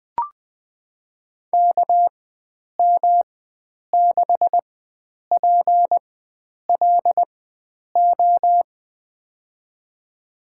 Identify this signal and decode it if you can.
Morse code, heard as a keyed tone, about 20 wpm, spaced as for 9 wpm, 700 Hz. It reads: KM6PLO
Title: USA Call Signs - 6 Characters - 4x Character Spacing - 20wpm